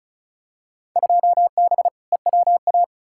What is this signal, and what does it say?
Morse code, heard as a keyed tone, about 34 wpm, 700 Hz. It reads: 2B EWA
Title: Field Day Exchanges Encore - 35wpm